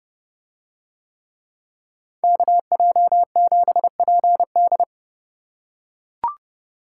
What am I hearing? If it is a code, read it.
Morse code, heard as a keyed tone, about 29 wpm, 700 Hz. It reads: KJ7PD